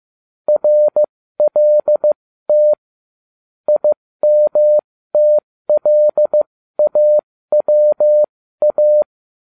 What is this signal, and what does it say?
Morse code, heard as a keyed tone, about 15 wpm, spaced as for 13 wpm, 600 Hz. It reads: RLT IMTLAWA
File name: cw / ubacw13 2